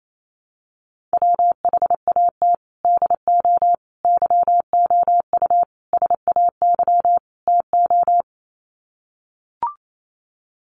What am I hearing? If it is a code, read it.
Morse code, heard as a keyed tone, about 28 wpm, 700 Hz. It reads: WHAT DO YOU SAY TO
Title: Sets of 5 Words 28wpm